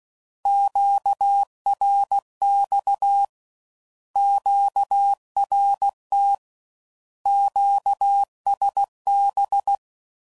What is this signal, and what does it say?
Morse code, heard as a keyed tone, about 16 wpm, 800 Hz. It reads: QRX QRT QSB
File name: Mixed / Misc / Words